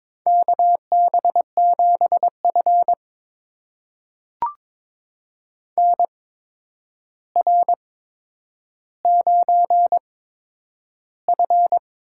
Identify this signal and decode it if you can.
Morse code, heard as a keyed tone, about 22 wpm, spaced as for 9 wpm, 700 Hz. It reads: KB7F NR9F
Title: USA Call Signs - 4 Characters - 8x Character Spacing - 22wpm